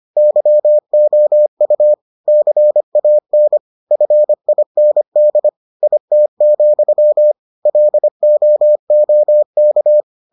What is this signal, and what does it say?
Morse code, heard as a keyed tone, about 25 wpm, 600 Hz. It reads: YOU CAN FIND IT, LOOK